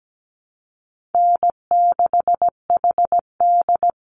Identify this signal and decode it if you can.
Morse code, heard as a keyed tone, about 17 wpm, 700 Hz. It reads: N6HD
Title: Call Signs Repeated 2x - 17wpm